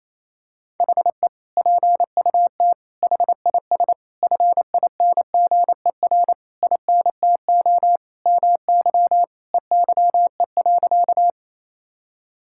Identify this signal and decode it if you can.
Morse code, heard as a keyed tone, about 27 wpm, 700 Hz. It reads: HE PUT HIS FINGER INTO MY EYE.
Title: Sentences from Top 1600 Words 28wpm